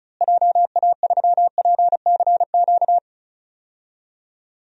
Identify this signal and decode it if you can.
Morse code, heard as a keyed tone, about 34 wpm, 700 Hz. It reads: JA3PCQ
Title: DXCC-100 Call Signs - 6 Characters - Repeated 3x - 35wpm